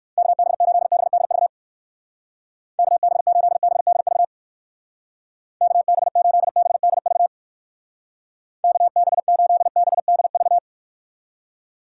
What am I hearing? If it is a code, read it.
Morse code, heard as a keyed tone, about 56 wpm, 700 Hz. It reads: KB8BDV KB8BDV KB8BDV KB8BDV
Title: Speed Racing - USA 2x3 Call Signs 45wpm